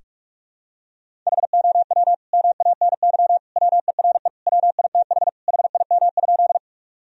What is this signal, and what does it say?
Morse code, heard as a keyed tone, about 46 wpm, 700 Hz. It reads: HOW MANY WERE WITH HIM?
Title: Sentences from Top 1000 Words - Review Part 1 - 45wpm